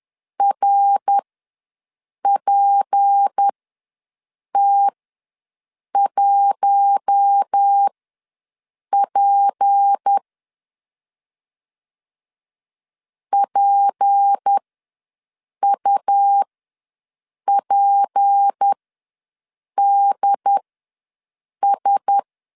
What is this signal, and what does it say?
Morse code, heard as a keyed tone, about 10 wpm, spaced as for 6 wpm, 800 Hz. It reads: RPT1P PUPDS
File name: LEZIONI / 6 PKX0 / 6 P ws 10wpm 3751